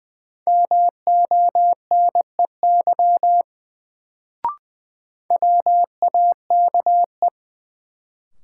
Morse code, heard as a keyed tone, about 21 wpm, 700 Hz. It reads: MONEY WAKE
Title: Sets of 4 Words - New Word Drill - 20wpm